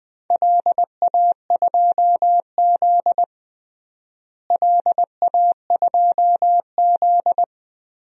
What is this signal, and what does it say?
Morse code, heard as a keyed tone, about 20 wpm, 700 Hz. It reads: LA2Z LA2Z
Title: DXCC-100 Call Signs - 4 Characters - Repeated 3x - 20wpm